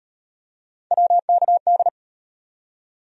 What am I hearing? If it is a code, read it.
Morse code, heard as a keyed tone, about 38 wpm, 700 Hz. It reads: WKD